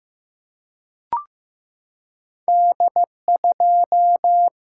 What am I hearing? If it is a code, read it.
Morse code, heard as a keyed tone, about 15 wpm, 700 Hz. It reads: D2
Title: Single-Letter Single-Number 15wpm